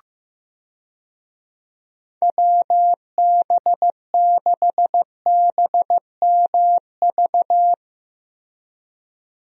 Morse code, heard as a keyed tone, about 15 wpm, 700 Hz. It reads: WB6BMV
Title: USA Call Signs - 6 Characters - Repeated 2x - 15wpm